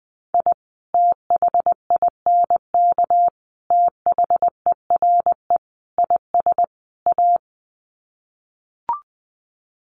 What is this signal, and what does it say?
Morse code, heard as a keyed tone, about 20 wpm, 700 Hz. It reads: I THINK THERE IS A
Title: Sets of 5 Words 20wpm